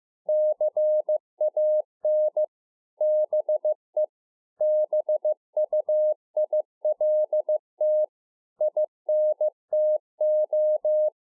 Morse code, heard as a keyed tone, about 15 wpm, 610 Hz. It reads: CAN BE BUILT INTO
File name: 2011-04-mp3 / A SMALL TRANCEIVER 2 15WPM